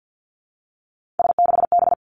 Morse code, heard as a keyed tone, about 50 wpm, 700 Hz. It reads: S6B
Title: Three Challenging Letter-Number Combinations 50wpm